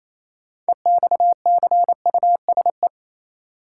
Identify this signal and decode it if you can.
Morse code, heard as a keyed tone, about 28 wpm, 700 Hz. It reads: EXCUSE
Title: Six Letter Words - 28wpm